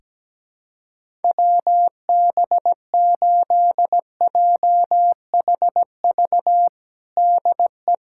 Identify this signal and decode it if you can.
Morse code, heard as a keyed tone, about 17 wpm, 700 Hz. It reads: WB8JHV DE
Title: USA Callsign DE Callsign 17wpm